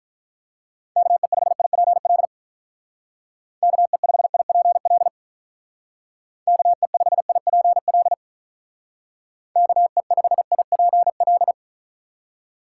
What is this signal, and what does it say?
Morse code, heard as a keyed tone, about 47 wpm, 700 Hz. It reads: KE5IPL KE5IPL KE5IPL KE5IPL